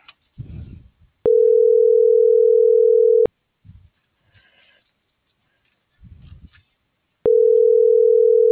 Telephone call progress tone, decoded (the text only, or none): ringback tone